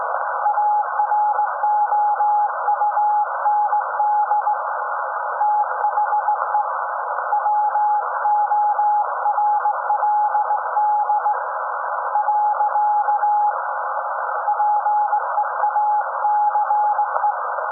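Morse code, heard as a keyed tone, about 19 wpm, 840 Hz. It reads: AROUND THE WORLD . PEACE
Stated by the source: SAQ Grimeton Radio, Alexanderson 200 kW alternator on 17.2 kHz, received in Potsdam with a very strong signal on World Radio Day.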